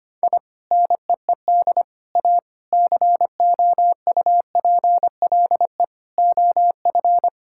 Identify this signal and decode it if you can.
Morse code, heard as a keyed tone, about 25 wpm, 700 Hz. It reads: I NEED A COUPLE OF
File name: practice / 25wpm / Sentences with 3-Word Phrases - 25wpm